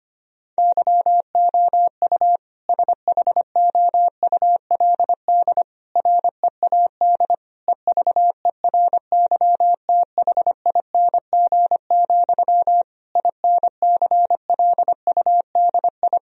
Morse code, heard as a keyed tone, about 25 wpm, 700 Hz. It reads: YOU SHOULD READ EVERYTHING, INCLUDI
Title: Sentences from Top 1000 Words - Review Part 4 - 25wpm